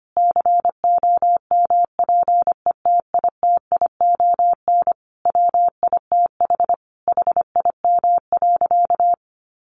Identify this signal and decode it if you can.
Morse code, heard as a keyed tone, about 25 wpm, 700 Hz. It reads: COMPETITION WITH HIM.
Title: Sentences from Top 1000 Words 25wpm